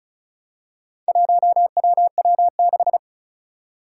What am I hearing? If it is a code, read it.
Morse code, heard as a keyed tone, about 36 wpm, 700 Hz. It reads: 1WW6
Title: Four Challenging Letter-Number Combinations 35wpm